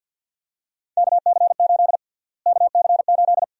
Morse code, heard as a keyed tone, about 52 wpm, 700 Hz. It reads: KC8 KC8
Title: USA Call Sign Prefixes - Repeated 2x - 50wpm